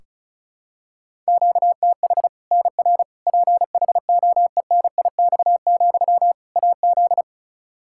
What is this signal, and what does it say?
Morse code, heard as a keyed tone, about 34 wpm, 700 Hz. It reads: QTH NR PHOENIX, AZ
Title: Practice QSO - Exchange 1 - Beginning - 35wpm